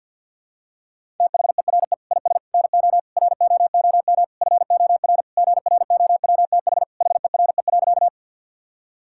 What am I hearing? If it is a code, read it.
Morse code, heard as a keyed tone, about 50 wpm, 690 Hz. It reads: THERE IS NO ROOM FOR GROWTH HERE.
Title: Sentences from Top 1000 Words - Review Part 5 - 50wpm